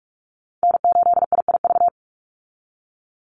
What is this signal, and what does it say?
Morse code, heard as a keyed tone, about 44 wpm, 700 Hz. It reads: N8IIV
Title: USA Call Signs 45wpm